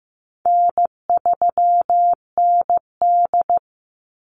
Morse code, heard as a keyed tone, about 15 wpm, 700 Hz. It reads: N3ND
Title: Easy Call Signs Repeated 3x - 15wpm